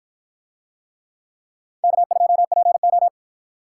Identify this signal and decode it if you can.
Morse code, heard as a keyed tone, about 52 wpm, 700 Hz. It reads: K2PO